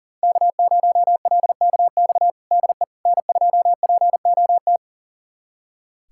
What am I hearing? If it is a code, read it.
Morse code, heard as a keyed tone, about 40 wpm, 700 Hz. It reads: K0LKX DE N2POT
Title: USA Callsign DE Callsign - 5 Characters - 40wpm